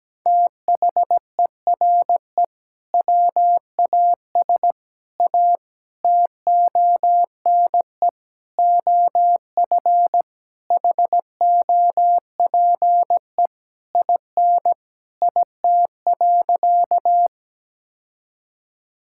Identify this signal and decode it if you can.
Morse code, heard as a keyed tone, about 17 wpm, 700 Hz. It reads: THERE WAS A TONE OF HOPE IN IT.
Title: Sentences from Top 1600 Words 17wpm